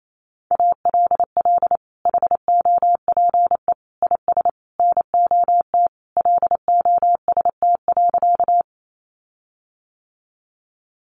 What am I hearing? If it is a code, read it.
Morse code, heard as a keyed tone, about 28 wpm, 700 Hz. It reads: ALL HOPE IS NOT LOST.
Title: Sentences from Top 1400 Words 28wpm